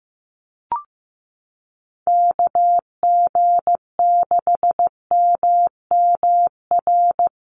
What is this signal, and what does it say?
Morse code, heard as a keyed tone, about 15 wpm, 700 Hz. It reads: KG6MMR
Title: USA Call Signs Repeated 3x 15wpm